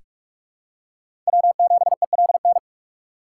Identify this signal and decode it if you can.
Morse code, heard as a keyed tone, about 45 wpm, 700 Hz. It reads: W7ELN
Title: USA Call Signs - 5 Characters - 3x Character Spacing - 45wpm